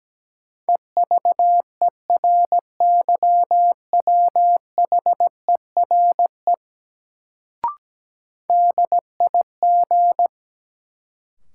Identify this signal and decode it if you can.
Morse code, heard as a keyed tone, about 17 wpm, 700 Hz. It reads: EVERYWHERE DIG
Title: Sentences from Top 1600 Words - New Word Drill Alt - 17wpm